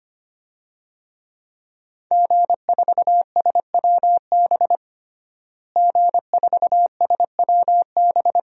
Morse code, heard as a keyed tone, about 25 wpm, 700 Hz. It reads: G4SWB G4SWB
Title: DXCC-100 Call Signs - 5 Characters - Repeated 2x - 25wpm